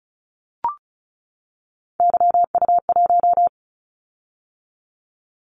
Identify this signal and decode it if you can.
Morse code, heard as a keyed tone, about 36 wpm, 690 Hz. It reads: YU1
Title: Callsign Prefixes for Top 25 DXCC - 35wpm